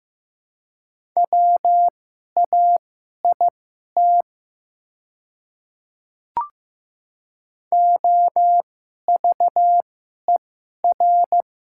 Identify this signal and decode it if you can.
Morse code, heard as a keyed tone, about 15 wpm, spaced as for 11 wpm, 690 Hz. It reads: WAIT OVER
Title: 2x Character Spacing - Top 500 Words - 15wpm